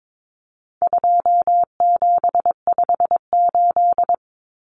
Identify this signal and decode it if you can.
Morse code, heard as a keyed tone, about 22 wpm, 700 Hz. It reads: 2758
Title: Numbers - 4-Digit 22wpm